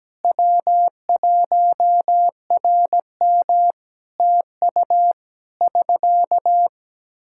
Morse code, heard as a keyed tone, about 17 wpm, 700 Hz. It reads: W1RM TU <SK>
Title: K1USN SST QSO Practice 17wpm